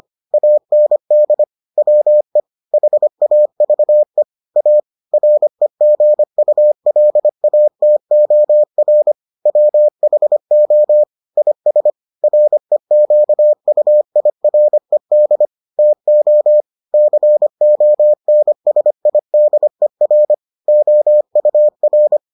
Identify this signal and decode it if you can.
Morse code, heard as a keyed tone, about 25 wpm, 600 Hz. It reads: AND WE HAVE A REGULATOR WHO IS REQUIRED TO CONSIDER OUR